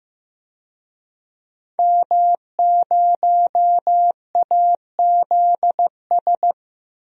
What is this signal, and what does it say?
Morse code, heard as a keyed tone, about 15 wpm, 700 Hz. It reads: M0AZS
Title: DXCC-100 Call Signs - 5 Characters - 15wpm